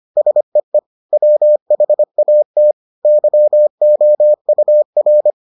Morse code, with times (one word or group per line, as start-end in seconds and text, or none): 0.2-0.8 SEE
1.1-2.7 WHAT
3.0-5.3 YOUR